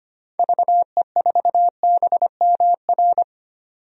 Morse code, heard as a keyed tone, about 25 wpm, 700 Hz. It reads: VE4BMR
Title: DXCC-100 Call Signs - 6 Characters - Repeated 3x - 25wpm